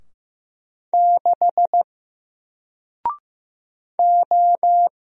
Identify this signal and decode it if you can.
Morse code, heard as a keyed tone, about 15 wpm, 700 Hz. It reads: 6 O
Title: Single Letter-Number Alt 15wpm